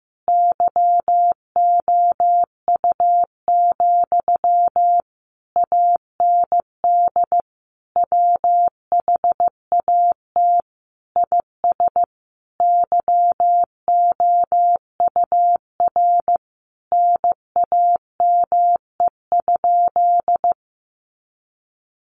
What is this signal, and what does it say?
Morse code, heard as a keyed tone, about 15 wpm, 700 Hz. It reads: YOU, AND WHAT IS YOUR NAME?
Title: Sentences from Top 1200 Words 15wpm